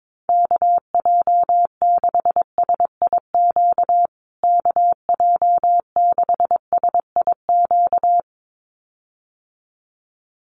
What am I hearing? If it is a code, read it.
Morse code, heard as a keyed tone, about 22 wpm, 700 Hz. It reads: KJ6SIQ KJ6SIQ
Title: USA Call Signs Repeated 2x 22wpm